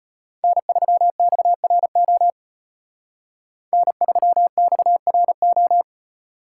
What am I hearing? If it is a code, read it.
Morse code, heard as a keyed tone, about 32 wpm, 700 Hz. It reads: N3XRO N3XRO